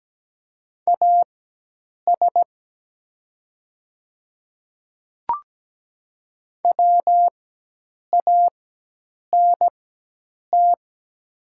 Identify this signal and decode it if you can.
Morse code, heard as a keyed tone, about 17 wpm, spaced as for 8 wpm, 700 Hz. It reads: AS WANT
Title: 4x Character Spacing - Top 300 Words - 17wpm